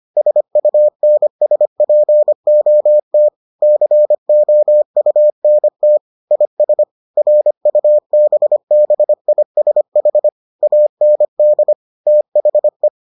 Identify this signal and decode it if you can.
Morse code, heard as a keyed tone, about 25 wpm, 600 Hz. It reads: SUNSPOT COUNT IS RUBBISH AND THE